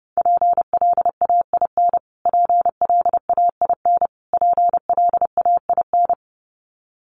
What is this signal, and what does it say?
Morse code, heard as a keyed tone, about 30 wpm, 700 Hz. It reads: PLAIN PLAIN PLAIN
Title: Top 500 Words Repeated 30wpm